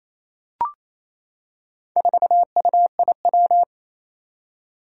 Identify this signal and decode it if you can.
Morse code, heard as a keyed tone, about 28 wpm, 700 Hz. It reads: 4UIW
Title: Four Challenging Letter-Number Combinations 28wpm